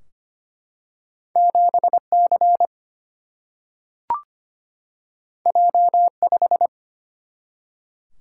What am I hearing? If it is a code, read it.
Morse code, heard as a keyed tone, about 25 wpm, 700 Hz. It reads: 7C J5
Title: Single-Letter Single-Number 25wpm